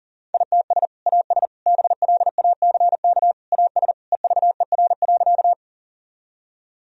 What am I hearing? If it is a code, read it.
Morse code, heard as a keyed tone, about 40 wpm, 700 Hz. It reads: ITS AS BLACK AS EVER.
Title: Sentences from Top 1000 Words - Review Part 2 - 40wpm